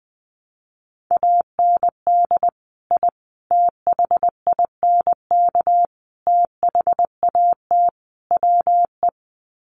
Morse code, heard as a keyed tone, about 20 wpm, 700 Hz. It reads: AND I THINK THAT WE
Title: Sets of 5 Words Encore2 20wpm